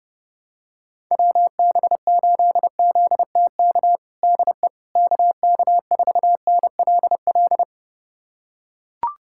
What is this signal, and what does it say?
Morse code, heard as a keyed tone, about 32 wpm, 700 Hz. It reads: WB8ZTK DE KK4NLL
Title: USA Callsign DE Callsign - 6 Characters - 30wpm